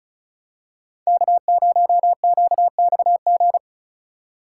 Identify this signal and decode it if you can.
Morse code, heard as a keyed tone, about 36 wpm, 700 Hz. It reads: K0QXG
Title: USA Call Signs - 5 Characters - 3x Character Spacing - 35wpm